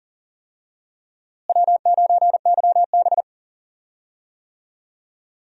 Morse code, heard as a keyed tone, about 39 wpm, 700 Hz. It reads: W9YB
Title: USA Call Signs - 4 Characters - Repeated 2x - 40wpm